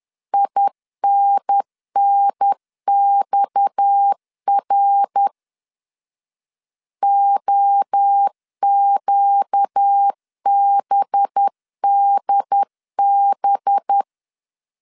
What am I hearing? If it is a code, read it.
Morse code, heard as a keyed tone, about 10 wpm, 800 Hz. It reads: INNXR OQBDB
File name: CW / DET / 1 6 1500C S B 10wpm 4137